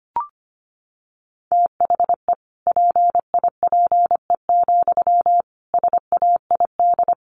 Morse code, heard as a keyed tone, about 25 wpm, 700 Hz. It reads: THE PIPE, SAID
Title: Sentences from Top 1500 Words 25wpm